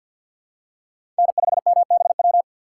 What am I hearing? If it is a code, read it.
Morse code, heard as a keyed tone, about 48 wpm, 700 Hz. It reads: N5MBW
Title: USA Call Signs - 5 Characters - 6x Character Spacing - 50wpm